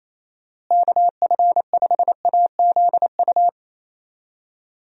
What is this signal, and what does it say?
Morse code, heard as a keyed tone, about 28 wpm, 700 Hz. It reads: KF5AZU
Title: USA Call Signs - 6 Characters - Repeated 2x - 28wpm